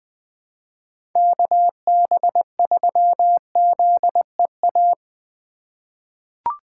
Morse code, heard as a keyed tone, about 21 wpm, 700 Hz. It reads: KB3ZEA
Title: USA Call Signs - 6 Characters - 4x Character Spacing - 20wpm